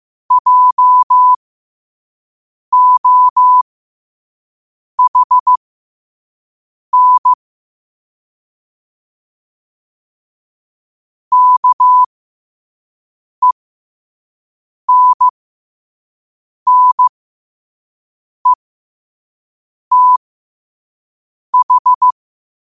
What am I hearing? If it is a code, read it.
Morse code, heard as a keyed tone, about 15 wpm, spaced as for 5 wpm, 1000 Hz. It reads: JOHN KENNETH
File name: qotdfiles / 5wpm